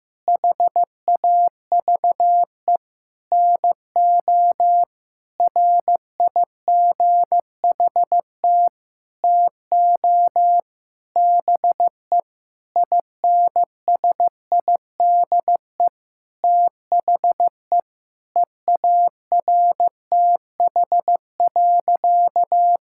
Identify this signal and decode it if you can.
Morse code, heard as a keyed tone, about 15 wpm, 700 Hz. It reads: HAVE NO RIGHT TO BE INSIDE THE EARTH.
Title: Sentences from Top 1000 Words - Review Part 6 - 15wpm